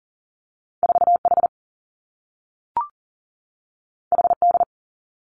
Morse code, heard as a keyed tone, about 40 wpm, 700 Hz. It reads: 4H HD
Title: Two Challenging Letter-Number Combinations 40wpm